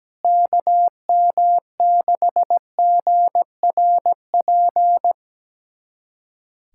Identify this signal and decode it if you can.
Morse code, heard as a keyed tone, about 17 wpm, 700 Hz. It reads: KM6GRP
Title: USA Callsign DE Callsign - 6 Characters - 17wpm